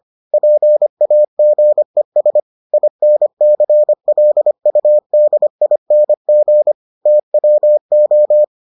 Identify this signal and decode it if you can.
Morse code, heard as a keyed tone, about 25 wpm, 600 Hz. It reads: PAGES INCLUDING TWO